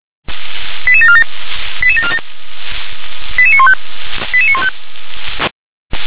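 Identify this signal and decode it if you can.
Five-tone selective call (ZVEI), heard as quick runs of five tones, tones of about 70 ms. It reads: 88747 88747 88915 88915
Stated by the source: ZVEI five-tone call received by SDR.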